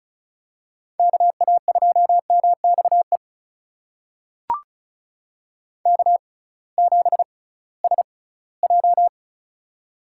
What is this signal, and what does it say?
Morse code, heard as a keyed tone, about 36 wpm, spaced as for 20 wpm, 700 Hz. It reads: KA2MXE K7SJ
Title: USA Call Signs - 6x Character Spacing - 35wpm